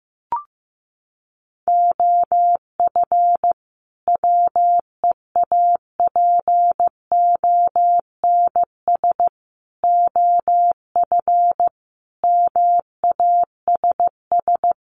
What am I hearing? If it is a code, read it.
Morse code, heard as a keyed tone, about 15 wpm, 700 Hz. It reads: OF WEAPONS OF MASS
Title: Sets of 5 Words Encore 15wpm